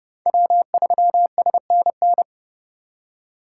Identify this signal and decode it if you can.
Morse code, heard as a keyed tone, about 29 wpm, 700 Hz. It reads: W3SNN